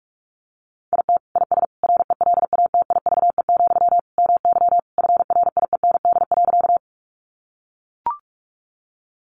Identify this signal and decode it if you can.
Morse code, heard as a keyed tone, about 44 wpm, 700 Hz. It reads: IT IS RELATIVE, MY FRIEND.
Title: Sentences from Top 1400 Words 45wpm